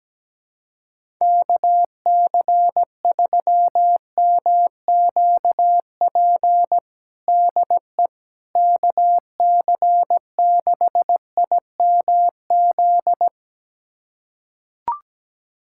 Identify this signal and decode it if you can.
Morse code, heard as a keyed tone, about 17 wpm, 700 Hz. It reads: KC3MQP DE KC6IMZ
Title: USA Callsign DE Callsign 17wpm